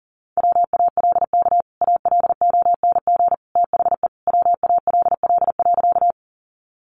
Morse code, heard as a keyed tone, about 40 wpm, 700 Hz. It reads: WALK ALONG THE WALL.
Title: Sentences from Top 1100 Words 40wpm